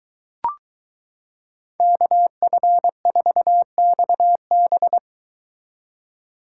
Morse code, heard as a keyed tone, about 23 wpm, 700 Hz. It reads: KF4XB